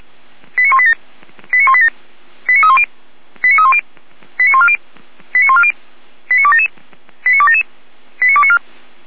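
Five-tone selective call (ZVEI), heard as quick runs of five tones, tones of about 70 ms. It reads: 89187 89187 89310 89310 89140 89140 89277 89277 89294
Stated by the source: ZVEI five-tone selective call, code 89229, received by SDR.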